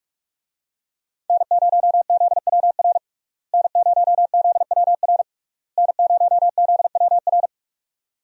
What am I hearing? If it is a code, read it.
Morse code, heard as a keyed tone, about 46 wpm, 700 Hz. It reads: N0ZWR N0ZWR N0ZWR
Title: USA Call Signs - 5 Characters - Repeated 3x - 45wpm